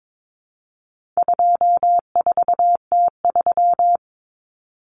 Morse code, heard as a keyed tone, about 22 wpm, 700 Hz. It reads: 24T3
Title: Common Cut Numbers - 4-Digit 22wpm